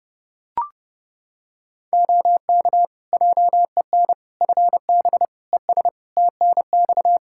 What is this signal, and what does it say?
Morse code, heard as a keyed tone, about 30 wpm, 700 Hz. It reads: OK JEN FB ES TNX
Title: Practice QSO - Exchange 2 - Info - 30wpm